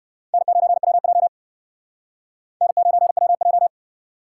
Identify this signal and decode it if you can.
Morse code, heard as a keyed tone, about 59 wpm, 700 Hz. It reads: N9WJ N9WJ